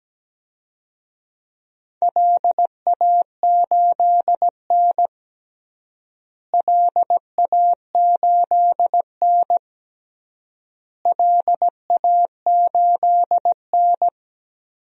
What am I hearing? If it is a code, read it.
Morse code, heard as a keyed tone, about 17 wpm, 700 Hz. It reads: LA8N LA8N LA8N
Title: DXCC-100 Call Signs - 4 Characters - Repeated 3x - 17wpm